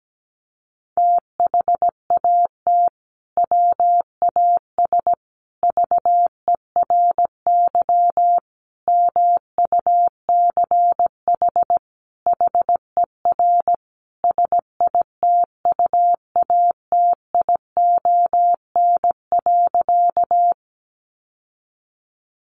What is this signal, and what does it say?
Morse code, heard as a keyed tone, about 17 wpm, 700 Hz. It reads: THAT WAS VERY MUCH HER SITUATION.
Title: Sentences from Top 1200 Words 17wpm